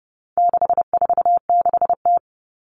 Morse code, heard as a keyed tone, about 29 wpm, 700 Hz. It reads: 646T
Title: All Cut Numbers - 4-Digit 30wpm